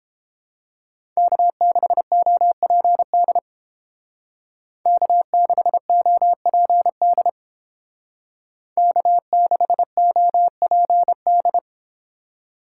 Morse code, heard as a keyed tone, about 30 wpm, 700 Hz. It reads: K6OPD K6OPD K6OPD